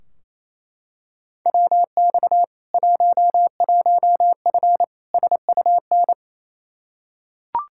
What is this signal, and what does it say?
Morse code, heard as a keyed tone, about 28 wpm, 700 Hz. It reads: WX 11F SUN
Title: Practice QSO - Exchange 2 - Info - 28wpm